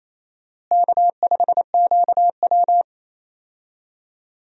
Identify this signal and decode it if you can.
Morse code, heard as a keyed tone, about 28 wpm, 700 Hz. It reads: K5QW
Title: USA Call Signs - 4 Characters - Repeated 3x - 28wpm